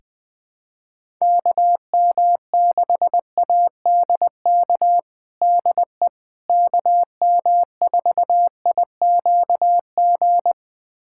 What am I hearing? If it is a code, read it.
Morse code, heard as a keyed tone, about 20 wpm, 700 Hz. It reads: KM6ADK DE KM4IQG
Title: USA Callsign DE Callsign - 6 Characters - 20wpm